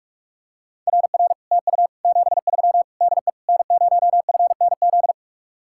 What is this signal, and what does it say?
Morse code, heard as a keyed tone, about 49 wpm, 700 Hz. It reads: RR TU 73 DE N0FNZ